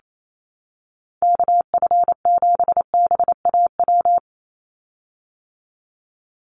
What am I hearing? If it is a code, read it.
Morse code, heard as a keyed tone, about 28 wpm, 700 Hz. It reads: KF7BAW
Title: USA Call Signs 28wpm